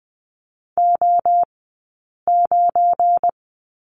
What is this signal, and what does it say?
Morse code, heard as a keyed tone, about 20 wpm, 700 Hz. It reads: O 9
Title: 190 Lesson 31d - Letter v - Two Characters 20wpm